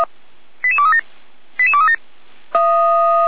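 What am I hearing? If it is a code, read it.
Five-tone selective call (ZVEI), heard as quick runs of five tones, tones of about 70 ms. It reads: 88237 88237
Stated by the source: ZVEI five-tone call received by SDR.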